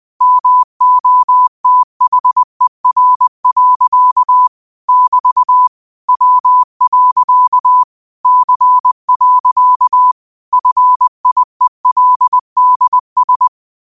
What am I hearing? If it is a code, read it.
Morse code, heard as a keyed tone, about 20 wpm, 1000 Hz. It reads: MOTHER. <BT> W. C. FIELDS
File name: 20wpm